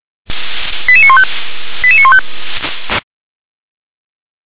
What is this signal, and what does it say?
Five-tone selective call (ZVEI), heard as quick runs of five tones, tones of about 70 ms. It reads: 88915 88915
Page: SDR/ZVEI